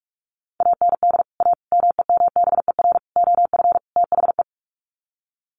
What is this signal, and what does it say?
Morse code, heard as a keyed tone, about 44 wpm, 700 Hz. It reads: AND A MEMBER OF THE
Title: Sets of 5 Words 45wpm